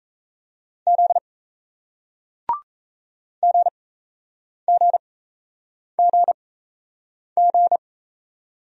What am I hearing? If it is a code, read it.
Morse code, heard as a keyed tone, about 38 wpm, spaced as for 8 wpm, 700 Hz. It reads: Z GGGG